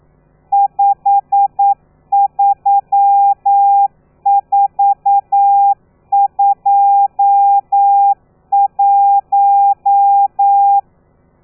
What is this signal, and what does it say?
Morse code, heard as a keyed tone, about 9 wpm, 800 Hz. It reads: 53421
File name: telegrafia / i2tfj / Lez12 40